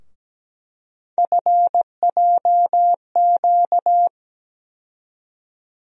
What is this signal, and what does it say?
Morse code, heard as a keyed tone, about 17 wpm, 700 Hz. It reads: FJQ
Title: USA Call Sign Postfixes - 3 Letters 17wpm